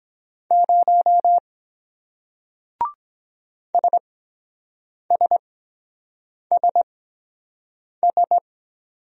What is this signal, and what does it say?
Morse code, heard as a keyed tone, about 23 wpm, spaced as for 7 wpm, 700 Hz. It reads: 0 SSSS